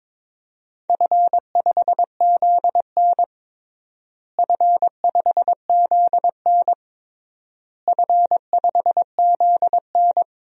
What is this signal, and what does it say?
Morse code, heard as a keyed tone, about 22 wpm, 700 Hz. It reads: F5ZN F5ZN F5ZN
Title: DXCC-100 Call Signs - 4 Characters - Repeated 3x - 22wpm